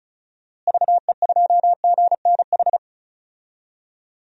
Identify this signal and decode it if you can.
Morse code, heard as a keyed tone, about 36 wpm, 700 Hz. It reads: VE2GNH
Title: DXCC-100 Call Signs - 6 Characters - Repeated 3x - 35wpm